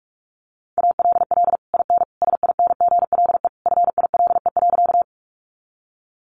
Morse code, heard as a keyed tone, about 44 wpm, 700 Hz. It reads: ALL IN SINGLE FILE.
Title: Sentences from Top 1200 Words 45wpm